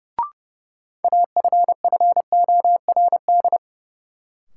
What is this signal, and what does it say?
Morse code, heard as a keyed tone, about 30 wpm, 700 Hz. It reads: AFFORD
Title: Sentences from Top 1100 Words - New Word Drill - 30wpm